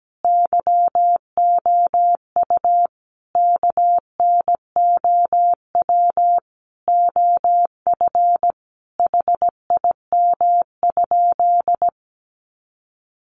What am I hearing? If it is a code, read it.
Morse code, heard as a keyed tone, about 17 wpm, 700 Hz. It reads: YOU KNOW OF HIM?
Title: Sentences from Top 100 Words 17wpm